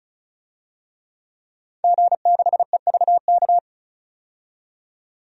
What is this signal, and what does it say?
Morse code, heard as a keyed tone, about 36 wpm, 700 Hz. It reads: G6EVK
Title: DXCC-100 Call Signs - 5 Characters - 35wpm